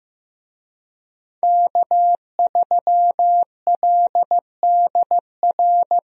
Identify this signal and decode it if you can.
Morse code, heard as a keyed tone, about 15 wpm, 700 Hz. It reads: K3LDR
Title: USA Call Signs - 5 Characters - Repeated 3x - 15wpm